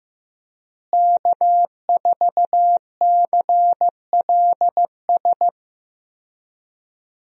Morse code, heard as a keyed tone, about 15 wpm, 700 Hz. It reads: K4CLS
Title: USA Call Signs - 5 Characters - Repeated 2x - 15wpm